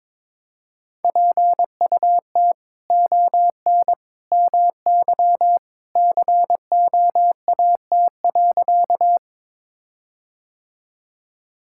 Morse code, heard as a keyed tone, about 22 wpm, 700 Hz. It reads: PUT ON MY COAT.
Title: Sentences from Top 1600 Words 22wpm